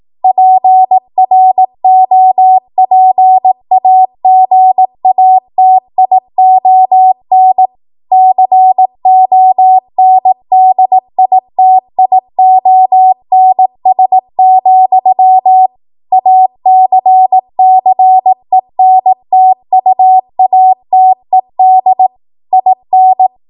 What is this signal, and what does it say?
Morse code, heard as a keyed tone, about 18 wpm, 750 Hz. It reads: PROPAGATION CONDITIONS, ACCENTUATED IN
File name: Morse / 18WPM